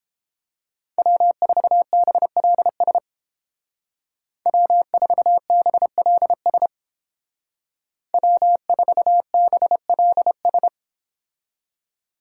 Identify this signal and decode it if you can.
Morse code, heard as a keyed tone, about 30 wpm, 700 Hz. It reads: W4BLS W4BLS W4BLS